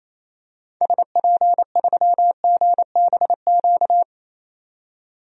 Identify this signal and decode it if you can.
Morse code, heard as a keyed tone, about 28 wpm, 700 Hz. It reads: SP3GBQ
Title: DXCC-100 Call Signs - 6 Characters - 28wpm